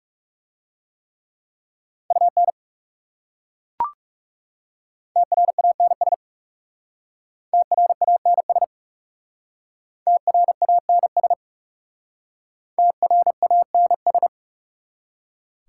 Morse code, heard as a keyed tone, about 39 wpm, 700 Hz. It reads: UN TRANS TRANS TRANS TRANS